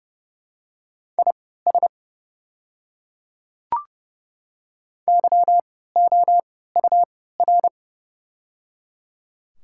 Morse code, heard as a keyed tone, about 30 wpm, spaced as for 17 wpm, 700 Hz. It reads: IS YOUR
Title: 3x Character Spacing - Top 200 Words - 30wpm